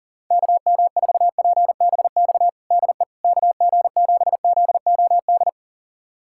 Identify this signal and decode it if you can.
Morse code, heard as a keyed tone, about 40 wpm, 690 Hz. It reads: KM4PBX DE KG7ZOD
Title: USA Callsign DE Callsign - 6 Characters - 40wpm